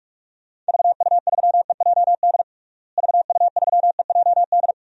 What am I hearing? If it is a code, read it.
Morse code, heard as a keyed tone, about 41 wpm, 700 Hz. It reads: VU3EJD VU3EJD
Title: DXCC-100 Call Signs - 6 Characters - Repeated 3x - 45wpm